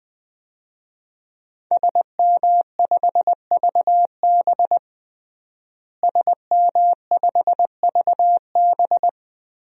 Morse code, heard as a keyed tone, about 20 wpm, 700 Hz. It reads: SM5VB SM5VB
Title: DXCC-100 Call Signs - 5 Characters - Repeated 3x - 20wpm